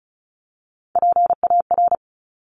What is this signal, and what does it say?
Morse code, heard as a keyed tone, about 36 wpm, 690 Hz. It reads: PAR
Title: Top Three Letter Combinations in Words 35wpm